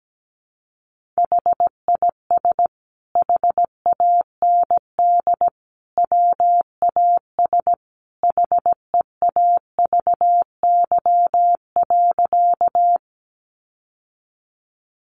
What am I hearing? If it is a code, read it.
Morse code, heard as a keyed tone, about 17 wpm, 700 Hz. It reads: HIS HAND WAS HEAVY.
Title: Sentences from Top 1100 Words 17wpm